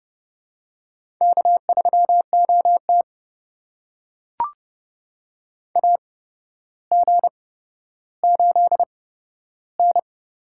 Morse code, heard as a keyed tone, about 31 wpm, spaced as for 13 wpm, 700 Hz. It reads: K3OT AG8N
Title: USA Call Signs - 4 Characters - 8x Character Spacing - 30wpm